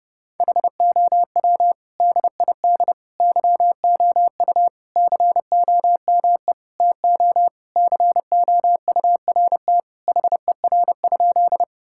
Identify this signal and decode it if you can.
Morse code, heard as a keyed tone, about 30 wpm, 700 Hz. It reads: HOW DID YOU COME TO COURT HER?
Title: Sentences from Top 1600 Words 30wpm